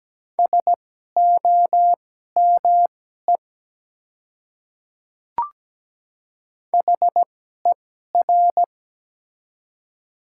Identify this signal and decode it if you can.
Morse code, heard as a keyed tone, about 17 wpm, spaced as for 12 wpm, 700 Hz. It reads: SOME HER
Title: 2x Character Spacing - Top 100 Words - 17wpm